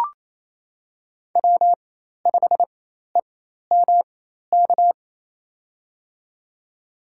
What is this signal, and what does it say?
Morse code, heard as a keyed tone, about 28 wpm, spaced as for 13 wpm, 700 Hz. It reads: W5EMK
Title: USA Call Signs - 5 Characters - 4x Character Spacing - 28wpm